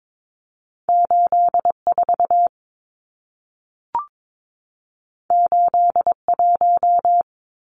Morse code, heard as a keyed tone, about 22 wpm, 700 Hz. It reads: 84 81
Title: Numbers - 2-Digit 22wpm